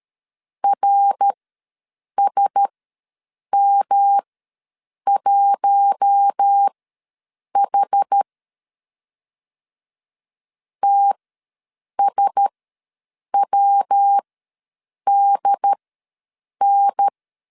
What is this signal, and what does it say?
Morse code, heard as a keyed tone, about 13 wpm, spaced as for 7 wpm, 800 Hz. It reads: RSM1H TSWDN
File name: LEZIONI / 5 LBVC / 5 L ws 12wpm 2958